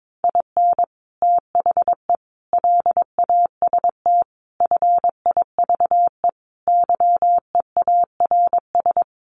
Morse code, heard as a keyed tone, about 22 wpm, 700 Hz. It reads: IN THE LAST FIVE YEARS
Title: Sets of 5 Words Encore 22wpm